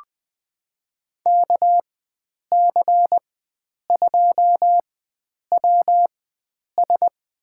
Morse code, heard as a keyed tone, about 20 wpm, spaced as for 9 wpm, 700 Hz. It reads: KC2WS
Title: USA Call Signs - 4x Character Spacing - 20wpm